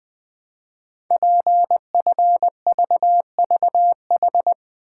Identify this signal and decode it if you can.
Morse code, heard as a keyed tone, about 20 wpm, 700 Hz. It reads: PFVVH
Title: Code Groups - 5 Letters - 20wpm